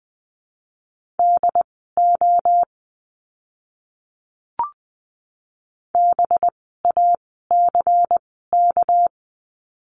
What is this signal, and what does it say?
Morse code, heard as a keyed tone, about 20 wpm, spaced as for 14 wpm, 700 Hz. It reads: DO BACK
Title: 230 Lesson 37f - Letter x - Words 20wpm